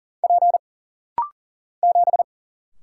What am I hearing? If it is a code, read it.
Morse code, heard as a keyed tone, about 39 wpm, 700 Hz. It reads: P 7
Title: Single Letter-Number - Rapid-Fire 40wpm